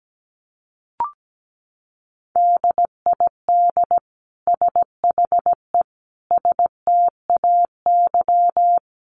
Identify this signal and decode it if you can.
Morse code, heard as a keyed tone, about 17 wpm, 700 Hz. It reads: DID SHE STAY
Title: Sentences from Top 500 Words 17wpm